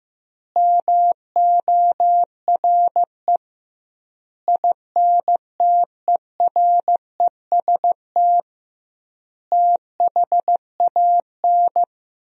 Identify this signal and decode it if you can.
Morse code, heard as a keyed tone, about 15 wpm, 700 Hz. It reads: MORE INTEREST THAN
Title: 2x Word-Spacing - Sentences from Top 400 Words - 15wpm